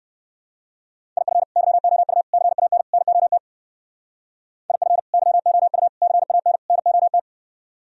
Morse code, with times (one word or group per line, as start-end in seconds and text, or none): 1.2-1.4 IF
1.6-2.2 YOU
2.3-2.8 CAN
2.9-3.4 NOT
4.7-5.0 IF
5.1-5.9 YOU
6.0-6.6 CAN
6.7-7.2 NOT